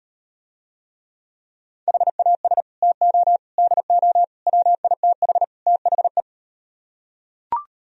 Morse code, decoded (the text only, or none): HAS TO DO WITH THE